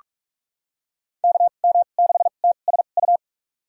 Morse code, heard as a keyed tone, about 46 wpm, spaced as for 33 wpm, 690 Hz. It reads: KM6TSU
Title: USA Call Signs - 6 Characters - 2x Character Spacing - 45wpm